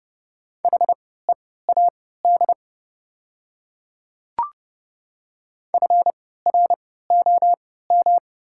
Morse code, heard as a keyed tone, about 31 wpm, spaced as for 17 wpm, 690 Hz. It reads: HEAD FROM
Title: 3x Character Spacing - Top 300 Words - 30wpm